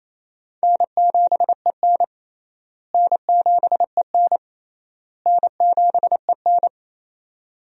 Morse code, heard as a keyed tone, about 27 wpm, 700 Hz. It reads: N7EN N7EN N7EN
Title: USA Call Signs - 4 Characters - Repeated 3x - 28wpm